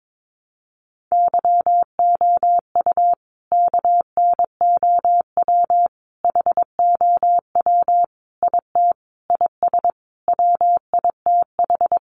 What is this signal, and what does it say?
Morse code, heard as a keyed tone, about 22 wpm, 700 Hz. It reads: YOU KNOW HOW IT IS WITH
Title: Sentences from Top 100 Words 22wpm